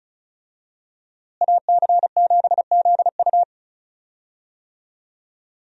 Morse code, heard as a keyed tone, about 36 wpm, 700 Hz. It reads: AC7ZU
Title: USA Call Signs - 5 Characters - Repeated 3x - 35wpm